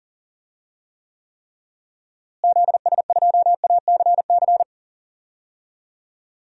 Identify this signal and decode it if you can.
Morse code, heard as a keyed tone, about 40 wpm, 700 Hz. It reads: ZS2ACC
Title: DXCC-100 Call Signs - 6 Characters - Repeated 3x - 40wpm